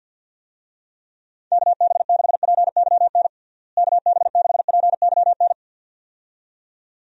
Morse code, heard as a keyed tone, about 46 wpm, 700 Hz. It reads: KB6PYN KB6PYN
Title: USA Call Signs - 6 Characters - Repeated 2x - 50wpm